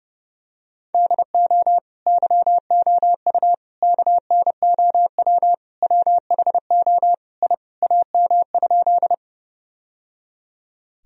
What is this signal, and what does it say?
Morse code, heard as a keyed tone, about 30 wpm, 700 Hz. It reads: DO YOU KNOW WHO I AM?
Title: Sentences from Top 1000 Words - Review Part 2 - 30wpm